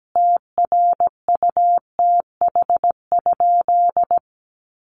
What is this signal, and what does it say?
Morse code, heard as a keyed tone, about 17 wpm, 700 Hz. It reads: TRUTH?
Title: Sentences from Top 1300 Words 17wpm